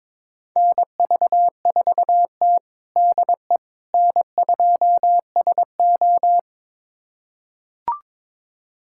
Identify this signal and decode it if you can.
Morse code, heard as a keyed tone, about 21 wpm, 700 Hz. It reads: NV4T DE N2SO
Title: USA Callsign DE Callsign - 4 Characters - 22wpm